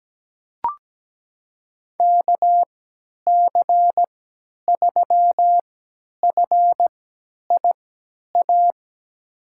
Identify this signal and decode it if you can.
Morse code, heard as a keyed tone, about 17 wpm, spaced as for 10 wpm, 700 Hz. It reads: KC3FIA
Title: USA Call Signs - 6 Characters - 3x Character Spacing - 17wpm